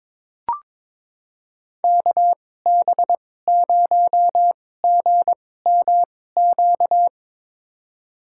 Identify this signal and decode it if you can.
Morse code, heard as a keyed tone, about 22 wpm, spaced as for 16 wpm, 700 Hz. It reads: KB0GMQ
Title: USA Call Signs - 2x Character Spacing - 22wpm